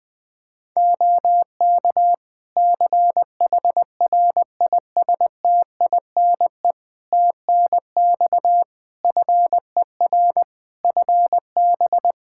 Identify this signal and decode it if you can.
Morse code, heard as a keyed tone, about 19 wpm, 700 Hz. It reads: OK CHRISTINE TNX FER FB
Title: Practice QSO - Exchange 4 - Ending - 20wpm